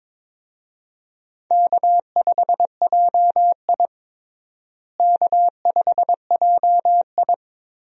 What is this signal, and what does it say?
Morse code, heard as a keyed tone, about 22 wpm, 700 Hz. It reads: K5JI K5JI
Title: USA Call Signs - 4 Characters - Repeated 3x - 22wpm